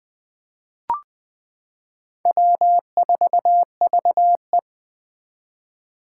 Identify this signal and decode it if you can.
Morse code, heard as a keyed tone, about 20 wpm, 700 Hz. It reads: W4VE
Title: Four Challenging Letter-Number Combinations 20wpm